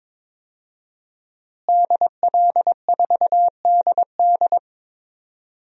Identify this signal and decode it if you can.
Morse code, heard as a keyed tone, about 22 wpm, 700 Hz. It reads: DL4DD
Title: DXCC-100 Call Signs - 5 Characters - Repeated 2x - 22wpm